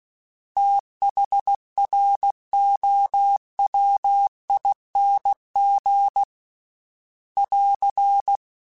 Morse code, heard as a keyed tone, about 16 wpm, 800 Hz. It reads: THROWING <AR>
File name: THs / TH / TH Prefixes 3x2 @15